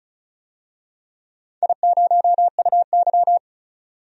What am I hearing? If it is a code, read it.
Morse code, heard as a keyed tone, about 34 wpm, 700 Hz. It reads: I0UY
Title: DXCC-100 Call Signs - 4 Characters - 35wpm